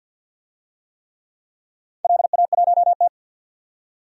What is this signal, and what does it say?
Morse code, heard as a keyed tone, about 53 wpm, 700 Hz. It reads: LA1T
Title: DXCC-100 Call Signs - 4 Characters - 50wpm